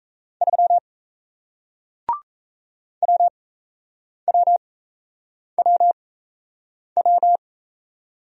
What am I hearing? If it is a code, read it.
Morse code, heard as a keyed tone, about 38 wpm, spaced as for 8 wpm, 700 Hz. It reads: 3 WWWW